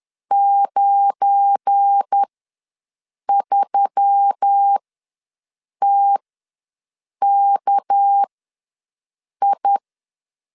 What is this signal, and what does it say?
Morse code, heard as a keyed tone, about 10 wpm, spaced as for 6 wpm, 800 Hz. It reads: 93TKI
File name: LEZIONI / 7 GF9Z / 7 GF9 ws 10wpm 4034